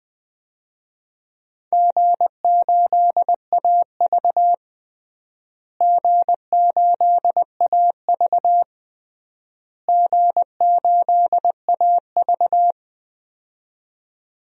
Morse code, heard as a keyed tone, about 20 wpm, 700 Hz. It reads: G8AV G8AV G8AV
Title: DXCC-100 Call Signs - 4 Characters - Repeated 3x - 20wpm